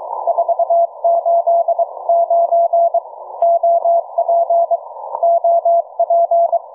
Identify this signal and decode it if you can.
Morse code, heard as a keyed tone, about 22 wpm, 680 Hz. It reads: 489 OP OP
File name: cw forum / 9m2ax 160m1829 22h14